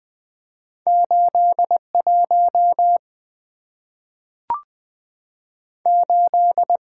Morse code, heard as a keyed tone, about 20 wpm, 700 Hz. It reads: 8 1 8
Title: Two Letter-Number Combinations - 20wpm